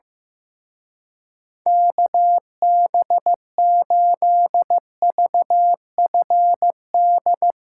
Morse code, heard as a keyed tone, about 16 wpm, 700 Hz. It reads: KB8VFD
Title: Speed Racing - USA 2x3 Call Signs 15wpm